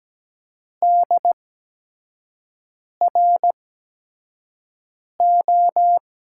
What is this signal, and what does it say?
Morse code, heard as a keyed tone, about 17 wpm, 700 Hz. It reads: D R O
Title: USA Call Signs - 6 Characters - 8x Character Spacing - 17wpm